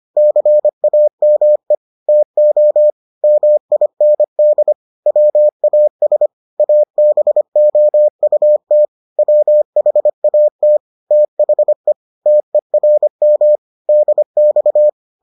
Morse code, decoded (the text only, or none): CAME TO MIND WAS ABOUT WHAT THE TERM DX